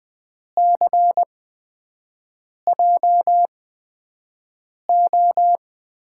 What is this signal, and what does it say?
Morse code, heard as a keyed tone, about 20 wpm, spaced as for 5 wpm, 700 Hz. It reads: CJO
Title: USA Call Signs - 6 Characters - 8x Character Spacing - 20wpm